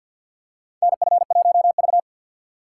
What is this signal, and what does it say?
Morse code, heard as a keyed tone, about 47 wpm, 700 Hz. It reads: NF1V
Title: USA Call Signs - 6x Character Spacing - 50wpm